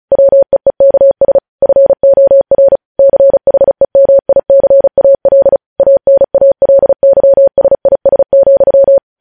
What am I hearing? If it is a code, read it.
Morse code, heard as a keyed tone, about 36 wpm, 550 Hz. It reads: WEEKS FOR CHEMICAL ANALYSIS,